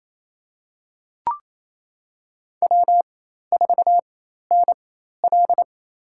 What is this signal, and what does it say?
Morse code, heard as a keyed tone, about 27 wpm, spaced as for 13 wpm, 700 Hz. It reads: W4NL
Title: USA Call Signs - 4 Characters - 4x Character Spacing - 28wpm